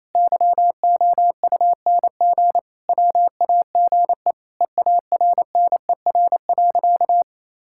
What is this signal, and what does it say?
Morse code, heard as a keyed tone, about 28 wpm, 700 Hz. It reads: YOUNG WAGE EARNER.
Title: Pangram Challenge v2 - 28wpm